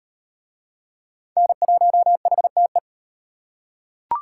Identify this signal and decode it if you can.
Morse code, heard as a keyed tone, about 33 wpm, 700 Hz. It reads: N1HTE